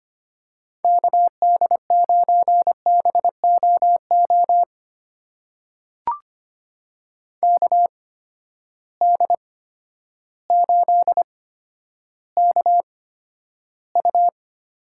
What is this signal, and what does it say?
Morse code, heard as a keyed tone, about 25 wpm, spaced as for 11 wpm, 700 Hz. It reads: KD9BOO KD8KU
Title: USA Call Signs - 6 Characters - 8x Character Spacing - 25wpm